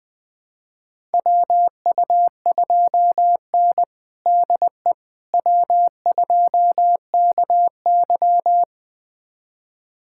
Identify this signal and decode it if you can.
Morse code, heard as a keyed tone, about 20 wpm, 700 Hz. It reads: WU2N DE W2KY
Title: USA Callsign DE Callsign - 4 Characters - 20wpm